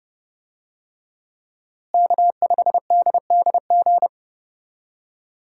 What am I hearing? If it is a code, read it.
Morse code, heard as a keyed tone, about 32 wpm, 700 Hz. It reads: K5DDG
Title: USA Call Signs - 5 Characters - Repeated 2x - 30wpm